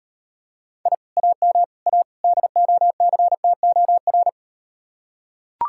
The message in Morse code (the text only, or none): I AM A DOCTOR